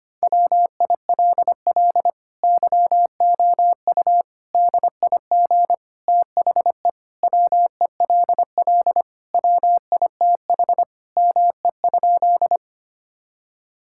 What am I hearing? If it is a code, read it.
Morse code, heard as a keyed tone, about 25 wpm, 700 Hz. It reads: WILL YOU DIG THE WELL WITH ME?
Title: Sentences from Top 1600 Words 25wpm